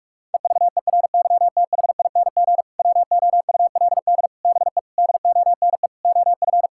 Morse code, heard as a keyed tone, about 46 wpm, 700 Hz. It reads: EVERYTHING WOULD BE DONE OF
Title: Sentences from Top 1000 Words - Review Part 5 - 45wpm